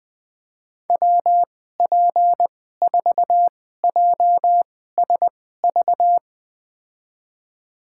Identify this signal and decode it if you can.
Morse code, heard as a keyed tone, about 20 wpm, spaced as for 15 wpm, 700 Hz. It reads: WP4JSV
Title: USA Call Signs - 6 Characters - 2x Character Spacing - 20wpm